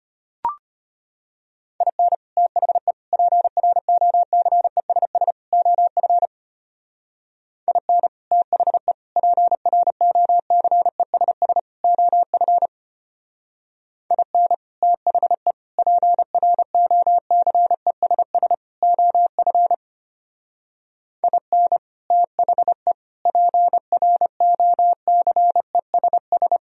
Morse code, heard as a keyed tone, about 32 wpm, 690 Hz. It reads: IN THE PROCESS OF IN THE PROCESS OF IN THE PROCESS OF IN THE PROCESS